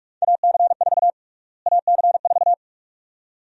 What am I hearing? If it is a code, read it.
Morse code, heard as a keyed tone, about 46 wpm, 690 Hz. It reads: AC4 AC4
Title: USA Call Sign Prefixes - Repeated 2x - 45wpm